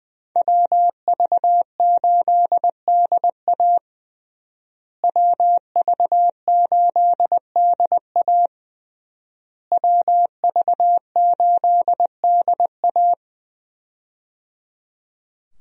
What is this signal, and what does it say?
Morse code, heard as a keyed tone, about 21 wpm, 700 Hz. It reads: WV8DA WV8DA WV8DA
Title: USA Call Signs - 5 Characters - Repeated 3x - 20wpm